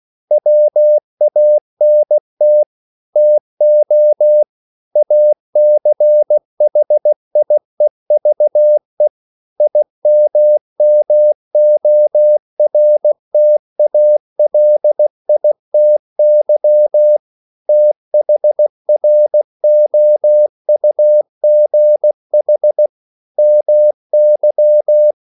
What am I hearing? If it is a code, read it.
Morse code, heard as a keyed tone, about 16 wpm, 600 Hz. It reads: WANT TO ACHIEVE IMMORTALITY THROUGH MY
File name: CW / Practice / EnglishQuotations0416